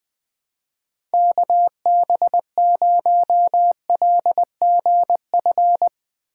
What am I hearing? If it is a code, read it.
Morse code, heard as a keyed tone, about 20 wpm, 700 Hz. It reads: KB0LGF
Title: K1USN SST QSO Practice - 20wpm